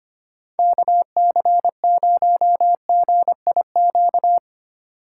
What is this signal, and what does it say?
Morse code, heard as a keyed tone, about 25 wpm, 700 Hz. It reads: KC0GIQ
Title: USA Call Signs - 6 Characters - 2x Character Spacing - 25wpm